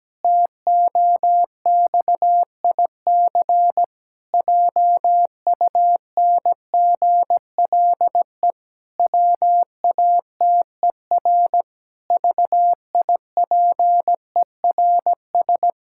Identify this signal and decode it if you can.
Morse code, heard as a keyed tone, about 17 wpm, 700 Hz. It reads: TOXIC JUNGLE WATER VIPERS
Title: Pangram Challenge v2 - 17wpm